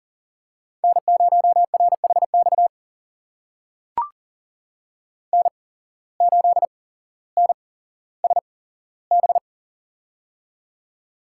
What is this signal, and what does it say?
Morse code, heard as a keyed tone, about 40 wpm, spaced as for 17 wpm, 700 Hz. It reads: N0RHX N8NSB
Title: USA Call Signs - 5 Characters - 8x Character Spacing - 40wpm